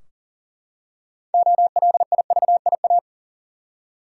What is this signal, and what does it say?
Morse code, heard as a keyed tone, about 40 wpm, 700 Hz. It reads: OLIVIA
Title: USA Common Names - 40wpm